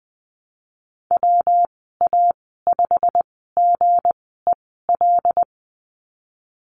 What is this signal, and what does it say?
Morse code, heard as a keyed tone, about 20 wpm, spaced as for 15 wpm, 700 Hz. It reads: WA5GEL
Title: 173 Lesson 28g - Number 9 - Call Signs 20wpm